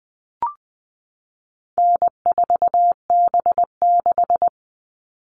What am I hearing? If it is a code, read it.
Morse code, heard as a keyed tone, about 20 wpm, 700 Hz. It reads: N4B6
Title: All Cut Numbers - 4-Digit 20wpm